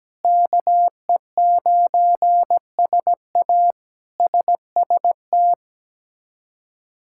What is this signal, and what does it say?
Morse code, heard as a keyed tone, about 17 wpm, 700 Hz. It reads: KE9SA SST
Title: K1USN SST QSO Practice - 17wpm